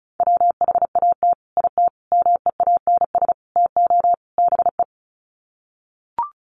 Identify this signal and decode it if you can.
Morse code, heard as a keyed tone, about 34 wpm, 700 Hz. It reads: WHAT IT MEANS TO BE
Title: Sets of 5 Words 35wpm